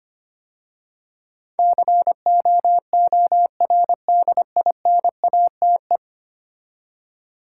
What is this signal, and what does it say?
Morse code, heard as a keyed tone, about 25 wpm, 700 Hz. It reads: COORDINATE
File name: 10 Letter Words - First 3 Letters Given - 25wpm